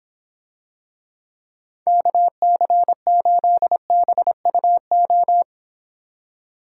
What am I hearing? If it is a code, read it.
Morse code, heard as a keyed tone, about 26 wpm, 700 Hz. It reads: KC8BUO